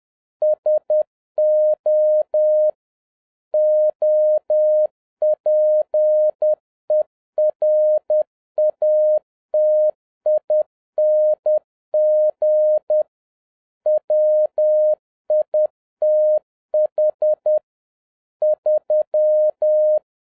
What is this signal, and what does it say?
Morse code, heard as a keyed tone, about 10 wpm, 610 Hz. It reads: SO OPERATING WITH 3